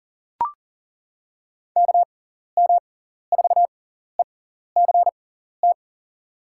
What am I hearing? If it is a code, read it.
Morse code, heard as a keyed tone, about 40 wpm, spaced as for 14 wpm, 700 Hz. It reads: KM4ECT
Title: USA Call Signs - 6 Characters - 6x Character Spacing - 40wpm